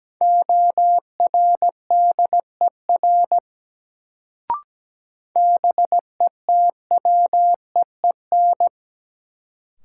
Morse code, heard as a keyed tone, about 17 wpm, 700 Hz. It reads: ORDER BETWEEN
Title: Sentences from Top 400 Words - New Word Drill - 17wpm